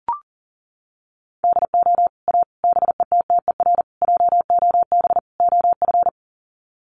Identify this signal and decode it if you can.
Morse code, heard as a keyed tone, about 40 wpm, 700 Hz. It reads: DO A BETTER JOB OF
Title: Sets of 5 Words Encore2 40wpm